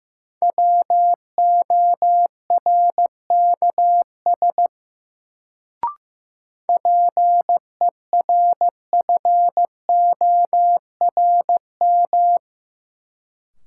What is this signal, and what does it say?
Morse code, heard as a keyed tone, about 15 wpm, 700 Hz. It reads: WORKS PERFORM
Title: Sentences from Top 1000 Words - Review Part 5 - New Word Drill - 15wpm